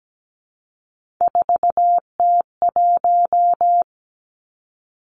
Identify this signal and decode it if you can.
Morse code, heard as a keyed tone, about 17 wpm, 700 Hz. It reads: 4T1
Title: Three Challenging Letter-Number Combinations 17wpm